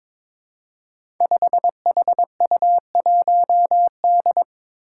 Morse code, heard as a keyed tone, about 22 wpm, 700 Hz. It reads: 5HU1D
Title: Five Challenging Letter-Number Combinations 22wpm